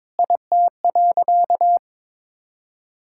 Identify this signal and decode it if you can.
Morse code, heard as a keyed tone, about 22 wpm, 700 Hz. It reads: IT.
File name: practice / 22wpm / Sentences with 8-Word Phrases - 22wpm